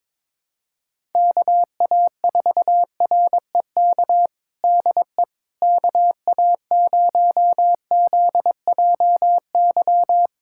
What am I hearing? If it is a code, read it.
Morse code, heard as a keyed tone, about 22 wpm, 700 Hz. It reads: KA4REK DE KA0ZJY
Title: USA Callsign DE Callsign 22wpm